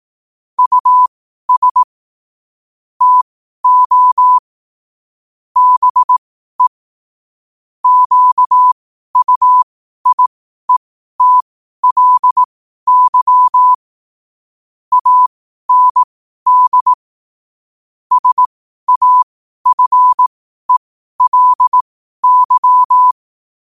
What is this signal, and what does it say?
Morse code, heard as a keyed tone, about 18 wpm, spaced as for 12 wpm, 1000 Hz. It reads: US TO BE QUIETLY AND SAFELY
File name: qotdfiles / 12wpm